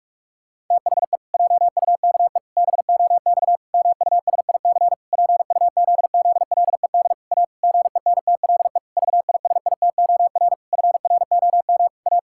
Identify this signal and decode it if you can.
Morse code, heard as a keyed tone, about 46 wpm, 700 Hz. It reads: THE JUKE BOX MUSIC PUZZLED A GENTLE VISITOR FROM A
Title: Pangram Challenge v2 - 45wpm